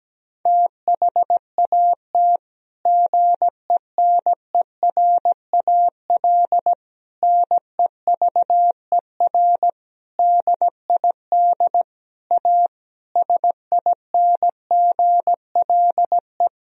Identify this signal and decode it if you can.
Morse code, heard as a keyed tone, about 17 wpm, 700 Hz. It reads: THAT GENERAL NEVER DID A SINGLE
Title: Sentences from Top 1000 Words - Review Part 4 - 17wpm